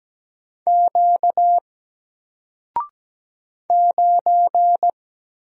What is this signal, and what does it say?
Morse code, heard as a keyed tone, about 17 wpm, 700 Hz. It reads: Q 9
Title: Single Letter-Number 17wpm